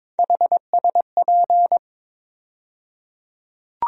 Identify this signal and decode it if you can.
Morse code, heard as a keyed tone, about 22 wpm, 700 Hz. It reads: HSP
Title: USA Call Sign Postfixes - 3 Letters 22wpm